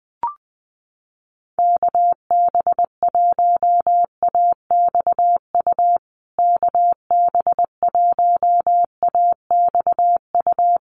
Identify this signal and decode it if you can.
Morse code, heard as a keyed tone, about 20 wpm, 700 Hz. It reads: KB1AXU KB1AXU
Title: USA Call Signs Repeated 3x 20wpm